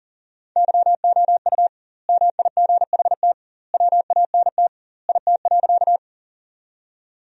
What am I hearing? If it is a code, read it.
Morse code, heard as a keyed tone, about 40 wpm, 700 Hz. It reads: YOU MIGHT WANT IT.
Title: 2x Word-Spacing - Sentences from Top 300 Words - 40wpm